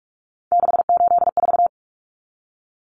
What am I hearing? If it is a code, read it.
Morse code, heard as a keyed tone, about 44 wpm, 700 Hz. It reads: 684
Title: Numbers - 3-Digit 45wpm